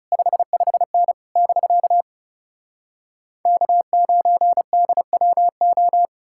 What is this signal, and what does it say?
Morse code, heard as a keyed tone, about 37 wpm, 700 Hz. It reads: 55N <BK> K9DWO
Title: Speed-Racing Common QSO Phrases Encore - 30wpm